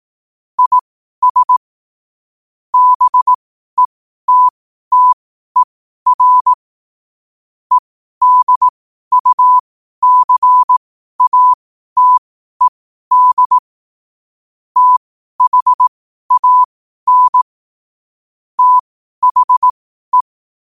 Morse code, heard as a keyed tone, about 18 wpm, spaced as for 12 wpm, 1000 Hz. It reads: IS BETTER EDUCATED THAN THE
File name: qotdfiles / 12wpm